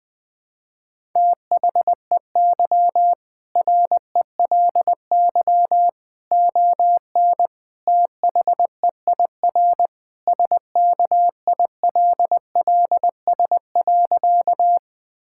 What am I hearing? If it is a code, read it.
Morse code, heard as a keyed tone, about 20 wpm, 700 Hz. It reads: THEY RELY ON THEIR SKILLS.
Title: Sentences with 2-Word Phrases - Spoken Beginning - 20wpm